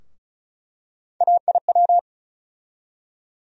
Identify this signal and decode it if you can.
Morse code, heard as a keyed tone, about 34 wpm, 700 Hz. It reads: AIW
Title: USA Call Sign Postfixes - 3 Letters 35wpm